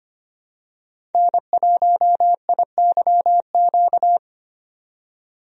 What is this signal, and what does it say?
Morse code, heard as a keyed tone, about 25 wpm, 700 Hz. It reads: N1IYQ
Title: USA Call Signs - 5 Characters - 25wpm